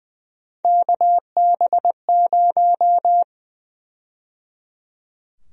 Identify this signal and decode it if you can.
Morse code, heard as a keyed tone, about 20 wpm, 700 Hz. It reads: KB0
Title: USA Call Sign Prefixes - 20wpm